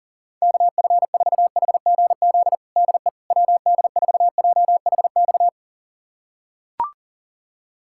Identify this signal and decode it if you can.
Morse code, heard as a keyed tone, about 40 wpm, 700 Hz. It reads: KF4HGZ DE WD4JHX
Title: USA Callsign DE Callsign - 6 Characters - 40wpm